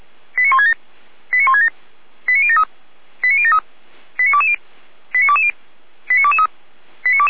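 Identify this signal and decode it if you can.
Five-tone selective call (ZVEI), heard as quick runs of five tones, tones of about 70 ms. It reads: 89167 89167 89073 89073 89220 89220 89223
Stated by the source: ZVEI five-tone call 89123 received by SDR, archived by Feuerwehr Orlamünde (fire brigade).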